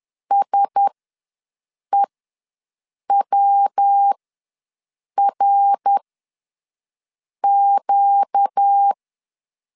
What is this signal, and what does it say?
Morse code, heard as a keyed tone, about 10 wpm, spaced as for 6 wpm, 800 Hz. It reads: SEWRQ
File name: LEZIONI / 8 YJ45 / 8 YJ ws 10wpm 4043